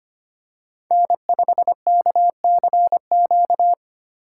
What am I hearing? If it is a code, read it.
Morse code, heard as a keyed tone, about 25 wpm, 700 Hz. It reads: N5KCQ
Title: USA Call Signs - 5 Characters - 4x Character Spacing - 25wpm